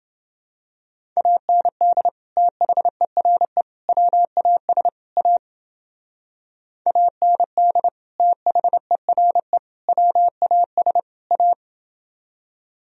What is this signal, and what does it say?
Morse code, heard as a keyed tone, about 31 wpm, 700 Hz. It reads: AND THERE WAS A AND THERE WAS A